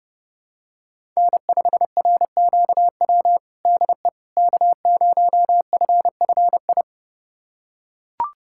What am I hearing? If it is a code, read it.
Morse code, heard as a keyed tone, about 30 wpm, 700 Hz. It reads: N5RQW DE K0FFI
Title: USA Callsign DE Callsign 30wpm